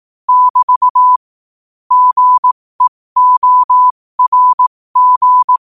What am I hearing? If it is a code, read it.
Morse code, heard as a keyed tone, about 18 wpm, spaced as for 15 wpm, 1000 Hz. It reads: <BT> GEORG